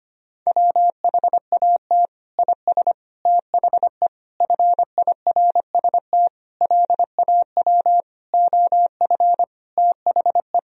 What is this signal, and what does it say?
Morse code, heard as a keyed tone, about 25 wpm, 700 Hz. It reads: WHAT IS THE FIRST LAW OF THE
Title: Sentences from Top 1000 Words - Review Part 3 - 25wpm